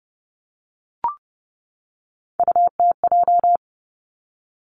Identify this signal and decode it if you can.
Morse code, heard as a keyed tone, about 31 wpm, 700 Hz. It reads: UTJ
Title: Three Challenging Letter-Number Combinations 30wpm